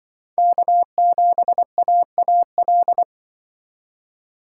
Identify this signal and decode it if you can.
Morse code, heard as a keyed tone, about 24 wpm, 700 Hz. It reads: K7AAL
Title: Speed Racing - USA 1x3 Call Signs 20wpm